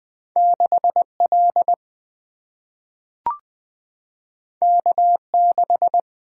Morse code, heard as a keyed tone, about 20 wpm, 700 Hz. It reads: 6L K6
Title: Single-Letter Single-Number 20wpm